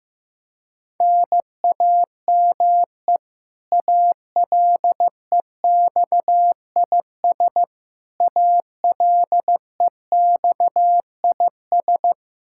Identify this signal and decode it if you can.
Morse code, heard as a keyed tone, about 15 wpm, 700 Hz. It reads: NAME ALEXIS ALEXIS
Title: Practice QSO - Exchange 1 - Beginning - 15wpm